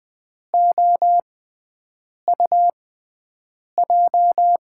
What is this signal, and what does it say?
Morse code, heard as a keyed tone, about 20 wpm, spaced as for 7 wpm, 700 Hz. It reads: OUJ
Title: USA Call Signs - 6x Character Spacing - 20wpm